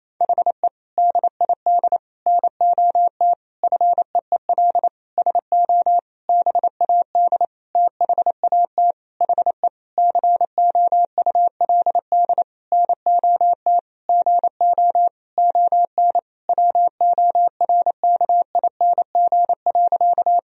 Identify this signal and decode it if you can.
Morse code, heard as a keyed tone, about 28 wpm, 700 Hz. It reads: HE DID NOT FEEL SO BAD THAT HE COULD NOT GO ON WORKING.
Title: Sentences from Top 1000 Words - Review Part 4 - 28wpm